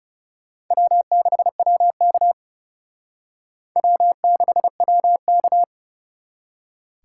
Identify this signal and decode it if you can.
Morse code, heard as a keyed tone, about 36 wpm, 700 Hz. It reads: W6WK W6WK
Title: Speed Racing - USA 1x2 Call Signs 30wpm